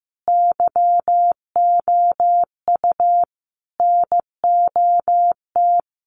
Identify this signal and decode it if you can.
Morse code, heard as a keyed tone, about 15 wpm, 700 Hz. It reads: YOU NOT
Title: Sentences from Top 1200 Words 15wpm